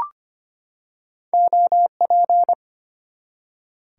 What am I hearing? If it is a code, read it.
Morse code, heard as a keyed tone, about 25 wpm, 700 Hz. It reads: OP
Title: Top 90 QSO Elements 25wpm